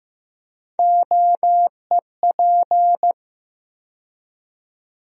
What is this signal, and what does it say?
Morse code, heard as a keyed tone, about 16 wpm, 700 Hz. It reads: OEP
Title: USA Call Sign Postfixes - 3 Letters 15wpm